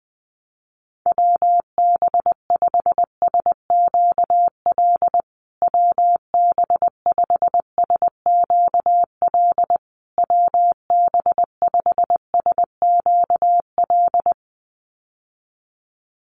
Morse code, heard as a keyed tone, about 20 wpm, 700 Hz. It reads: WB5SQL WB5SQL WB5SQL
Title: USA Call Signs Repeated 3x 20wpm